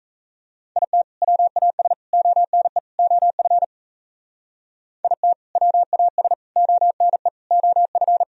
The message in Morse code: IT WAS ONE OF IT WAS ONE OF

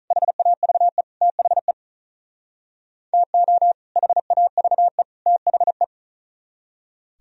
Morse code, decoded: HAVE THE TO HAVE THE

